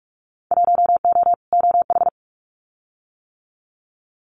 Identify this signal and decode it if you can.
Morse code, heard as a keyed tone, about 44 wpm, 700 Hz. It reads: 1O OH
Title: Winter Field Day Exchanges 45wpm